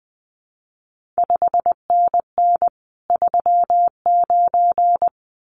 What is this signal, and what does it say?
Morse code, heard as a keyed tone, about 20 wpm, 700 Hz. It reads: 5NN 39
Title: CQ WW DX Contest Exchange - 20wpm